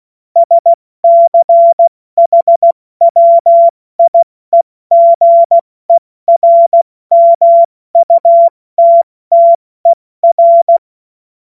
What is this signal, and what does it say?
Morse code, heard as a keyed tone, about 16 wpm, spaced as for 14 wpm, 680 Hz. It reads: SCHWIEGERMUTTER
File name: Morsekurs / Witze / Witz 07. Februar 2023 70 F80